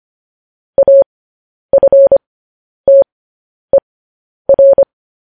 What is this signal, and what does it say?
Morse code, heard as a keyed tone, about 25 wpm, spaced as for 10 wpm, 550 Hz. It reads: AFTER
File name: podcast / News headlines-1x-25x10